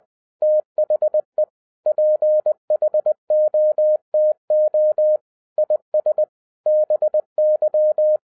Morse code, heard as a keyed tone, about 20 wpm, 610 Hz. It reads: THE PHOTO IS BY
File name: 2011-07-mp3 / n48 JOHN K3TN 20wpm